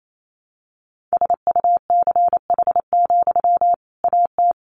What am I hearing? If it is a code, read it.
Morse code, heard as a keyed tone, about 28 wpm, 700 Hz. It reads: SUCH, AT
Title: Sentences from Top 1500 Words 28wpm